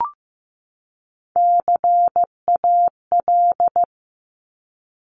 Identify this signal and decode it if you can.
Morse code, heard as a keyed tone, about 15 wpm, 700 Hz. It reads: CAL
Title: Top Three Letter Combinations in Words 15wpm